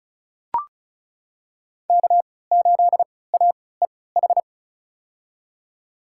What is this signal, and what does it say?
Morse code, heard as a keyed tone, about 36 wpm, spaced as for 20 wpm, 700 Hz. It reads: K8AEH
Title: USA Call Signs - 5 Characters - 3x Character Spacing - 35wpm